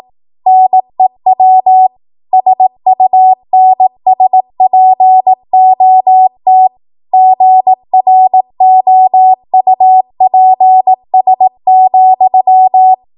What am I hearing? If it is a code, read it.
Morse code, heard as a keyed tone, about 18 wpm, 750 Hz. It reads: NEW SUNSPOT GROUPS,
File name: Archive / 18 WPM / 18WPM